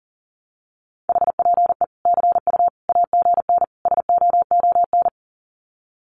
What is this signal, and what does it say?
Morse code, heard as a keyed tone, about 37 wpm, 700 Hz. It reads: HPE CU AGN SOON
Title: Common QSO Phrases 40wpm